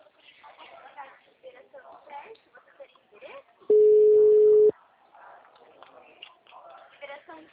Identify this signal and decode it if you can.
Ringback tone (Europe, 425 Hz), with speech.
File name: out- -2004- - - .6